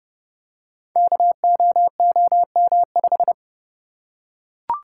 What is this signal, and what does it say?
Morse code, heard as a keyed tone, about 30 wpm, 700 Hz. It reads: KOOM5
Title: Code Groups - 5 Letter:Numbers - 30wpm